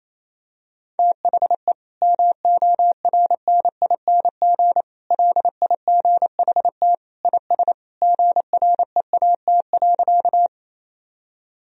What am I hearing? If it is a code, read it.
Morse code, heard as a keyed tone, about 28 wpm, 700 Hz. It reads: THE MORNING LIGHT IS GREAT.
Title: Sentences from Top 1000 Words - Review Part 1 - 28wpm